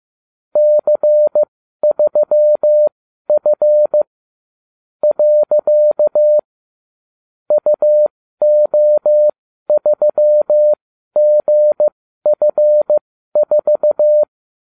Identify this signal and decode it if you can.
Morse code, heard as a keyed tone, about 15 wpm, spaced as for 12 wpm, 600 Hz. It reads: C3F . UO3GF4